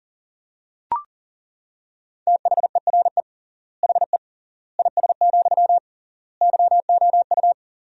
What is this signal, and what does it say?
Morse code, heard as a keyed tone, about 40 wpm, 700 Hz. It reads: THERE HE IS, YOU
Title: 3x Word-Spacing - Sentences from Top 100 Words - 40wpm